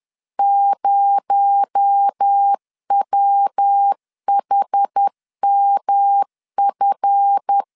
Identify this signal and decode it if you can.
Morse code, heard as a keyed tone, about 10 wpm, 800 Hz. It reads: 0WHMF